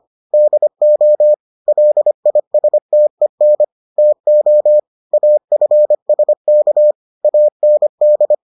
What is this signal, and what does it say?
Morse code, heard as a keyed tone, about 25 wpm, 600 Hz. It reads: DO LISTEN TO AFSK AND